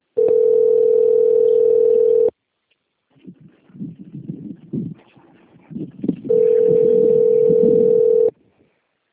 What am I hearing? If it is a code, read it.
Ringback tone (North America), 440 + 480 Hz.
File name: out- - - - - .91945